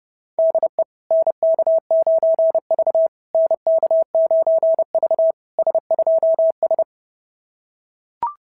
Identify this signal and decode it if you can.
Morse code, heard as a keyed tone, about 29 wpm, 690 Hz. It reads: DE NK9V NK9V S2S
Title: SOTA QSO Practice 30wpm